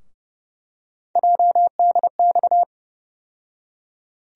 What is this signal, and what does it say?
Morse code, heard as a keyed tone, about 30 wpm, 700 Hz. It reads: JDX
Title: USA Call Sign Postfixes - 3 Letters 30wpm